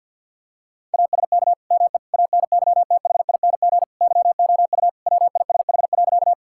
Morse code, heard as a keyed tone, about 50 wpm, 700 Hz. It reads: ASK ME ANYTHING YOU WISH.
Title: Sentences from Top 1600 Words 50wpm